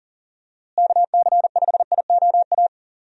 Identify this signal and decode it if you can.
Morse code, heard as a keyed tone, about 40 wpm, 700 Hz. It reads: KC5IOA
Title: USA Call Signs - 6 Characters - Repeated 3x - 40wpm